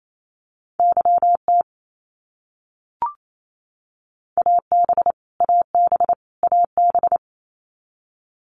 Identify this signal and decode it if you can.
Morse code, heard as a keyed tone, about 28 wpm, 700 Hz. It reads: YT AB AB AB
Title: Canadian Provinces and Territories - Repeated 3x - 28wpm